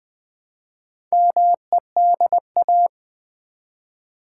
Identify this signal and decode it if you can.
Morse code, heard as a keyed tone, about 20 wpm, 700 Hz. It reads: MEDA
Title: Build A Word - 1 Letter At A Time - 5-Letter Words - 20wpm